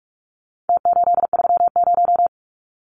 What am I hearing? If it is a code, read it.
Morse code, heard as a keyed tone, about 44 wpm, 700 Hz. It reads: T830
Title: Numbers - Contest Style - 4-Digit 45wpm